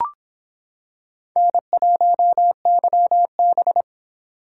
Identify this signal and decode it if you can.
Morse code, heard as a keyed tone, about 26 wpm, 700 Hz. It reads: N1YB